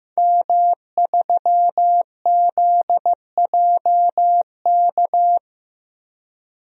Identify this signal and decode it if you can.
Morse code, heard as a keyed tone, about 15 wpm, 700 Hz. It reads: M3ZJK
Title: DXCC-100 Call Signs - 5 Characters - Repeated 2x - 15wpm